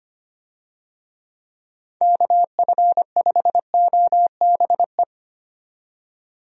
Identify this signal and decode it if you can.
Morse code, heard as a keyed tone, about 25 wpm, 700 Hz. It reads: KF5OBE
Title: USA Call Signs - 6 Characters - 25wpm